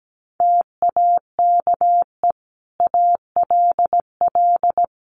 Morse code, heard as a keyed tone, about 17 wpm, 700 Hz. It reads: TAKE ALL
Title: Sentences from Top 900 Words 17wpm